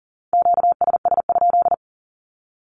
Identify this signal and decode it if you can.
Morse code, heard as a keyed tone, about 37 wpm, 700 Hz. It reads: QSS?
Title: Q-Codes with Spoken Meaning - 40wpm